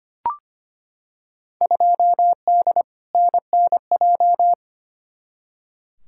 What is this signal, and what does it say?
Morse code, heard as a keyed tone, about 25 wpm, 700 Hz. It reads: 2D NNJ
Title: Field Day Exchanges - 25wpm